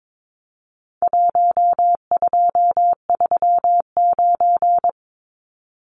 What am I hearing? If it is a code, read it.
Morse code, heard as a keyed tone, about 22 wpm, 700 Hz. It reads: 1239
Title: Numbers - 4-Digit 22wpm